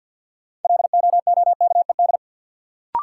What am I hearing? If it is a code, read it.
Morse code, heard as a keyed tone, about 46 wpm, 700 Hz. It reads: LOOKED
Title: Sentences from Top 800 Words - New Word Drill - 50wpm